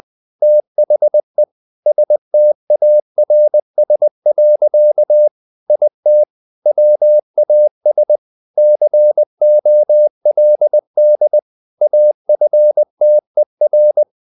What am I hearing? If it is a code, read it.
Morse code, heard as a keyed tone, about 20 wpm, 600 Hz. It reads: THE STARS. IT WAS COLD AFTER
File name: the old man and the sea / the old man and the sea part 1 20wpm